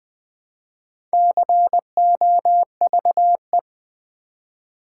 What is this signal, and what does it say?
Morse code, heard as a keyed tone, about 20 wpm, 700 Hz. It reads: COVE
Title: Build A Word - 2 Letters At A Time - 8-Letter Words - 20wpm